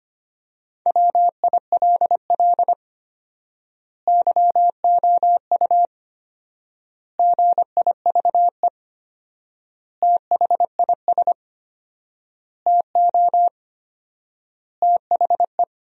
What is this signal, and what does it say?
Morse code, heard as a keyed tone, about 25 wpm, 700 Hz. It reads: WILL YOU GIVE THIS TO THE
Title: 4x Word-Spacing - Sentences from Top 200 Words - 25wpm